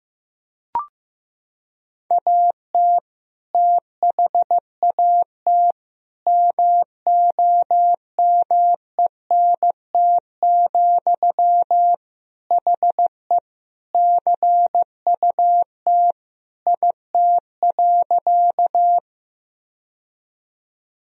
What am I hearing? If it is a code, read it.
Morse code, heard as a keyed tone, about 15 wpm, 700 Hz. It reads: AT THAT MOMENT, HE CUT IT.
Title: Sentences from Top 1000 Words - Review Part 1 - 15wpm